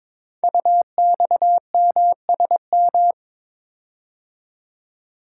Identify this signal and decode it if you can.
Morse code, heard as a keyed tone, about 22 wpm, 700 Hz. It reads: UXMSM
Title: Code Groups - 5 Letters - 22wpm